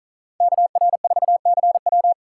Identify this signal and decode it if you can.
Morse code, heard as a keyed tone, about 46 wpm, 700 Hz. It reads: KR4CW